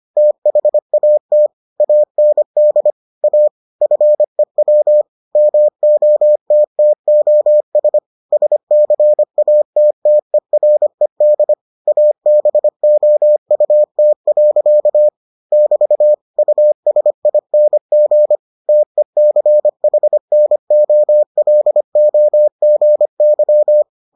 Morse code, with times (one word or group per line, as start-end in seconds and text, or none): 0.2-1.5 THAT
1.8-2.9 AND
3.2-3.5 A
3.8-5.0 FEW
5.4-8.0 MOTTOS
8.3-11.5 SCATTERED
11.9-15.1 ABOUT.
15.5-16.1 <BT>
16.4-18.4 USING
18.7-23.8 TECHNOLOGY